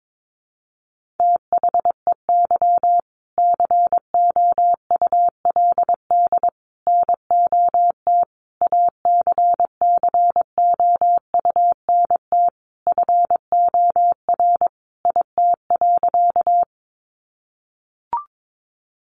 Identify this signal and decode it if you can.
Morse code, heard as a keyed tone, about 22 wpm, 700 Hz. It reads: THEY COULD NOT ACCOUNT FOR IT.
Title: Sentences from Top 900 Words 22wpm